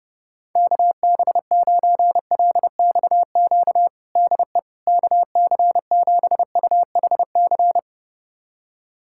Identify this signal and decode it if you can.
Morse code, heard as a keyed tone, about 30 wpm, 700 Hz. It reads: KB9LXQ DE KC7UHC
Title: USA Callsign DE Callsign - 6 Characters - 30wpm